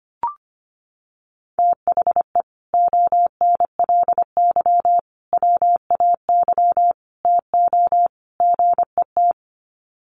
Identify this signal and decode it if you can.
Morse code, heard as a keyed tone, about 25 wpm, 700 Hz. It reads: THE ONLY WAY TO GET
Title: Sets of 5 Words Encore2 25wpm